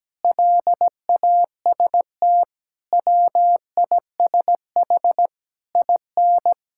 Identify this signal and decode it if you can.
Morse code, heard as a keyed tone, about 17 wpm, 700 Hz. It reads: LAST WISH IN
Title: Sentences from Top 1600 Words 17wpm